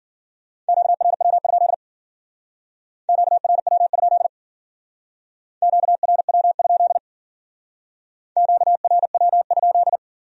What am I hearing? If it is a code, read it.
Morse code, heard as a keyed tone, about 52 wpm, 700 Hz. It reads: QRW? QRW? QRW? QRW?